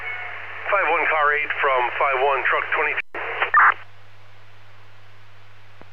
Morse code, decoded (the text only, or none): none